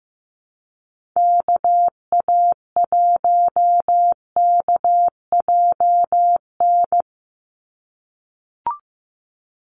Morse code, heard as a keyed tone, about 15 wpm, 700 Hz. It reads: KA1KJN
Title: USA Call Signs 15wpm